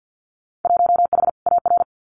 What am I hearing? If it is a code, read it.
Morse code, heard as a keyed tone, about 50 wpm, 700 Hz. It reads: 1H AR
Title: Winter Field Day Exchanges 50wpm